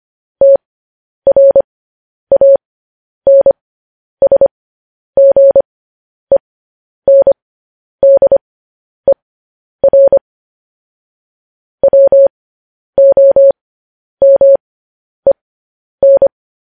Morse code, heard as a keyed tone, about 25 wpm, spaced as for 10 wpm, 550 Hz. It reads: TRANSGENDER WOMEN
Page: news-headlines-in-morse-code-at-10-wpm